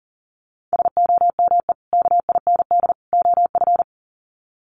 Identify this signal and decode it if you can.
Morse code, heard as a keyed tone, about 40 wpm, 700 Hz. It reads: SOME KIND OF
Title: Sets of 3 Words 40wpm